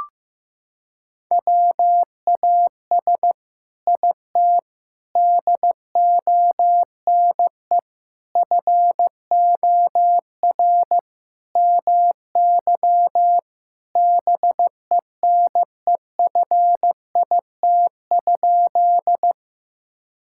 Morse code, heard as a keyed tone, about 15 wpm, 700 Hz. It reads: WAS IT DONE FOR MY BENEFIT?
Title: Sentences from Top 1000 Words - Review Part 6 - 15wpm